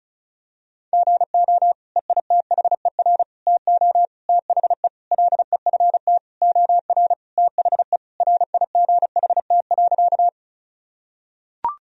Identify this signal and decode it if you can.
Morse code, heard as a keyed tone, about 36 wpm, 700 Hz. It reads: GO EITHER TO THE LEFT OR THE RIGHT.
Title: Sentences from Top 1000 Words - Review Part 4 - 35wpm